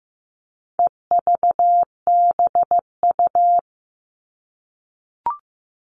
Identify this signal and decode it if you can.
Morse code, heard as a keyed tone, about 15 wpm, 700 Hz. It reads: EVBU
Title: All Cut Numbers - 4-Digit 15wpm